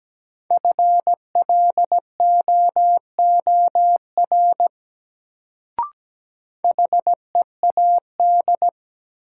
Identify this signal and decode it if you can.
Morse code, heard as a keyed tone, about 17 wpm, 700 Hz. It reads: FLOOR HEAD
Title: Sets of 3 Words - New Word Drill - 17wpm